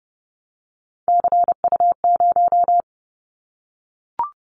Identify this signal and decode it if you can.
Morse code, heard as a keyed tone, about 30 wpm, 700 Hz. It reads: CU0
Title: Callsign Prefixes for Top 50 DXCC - 30wpm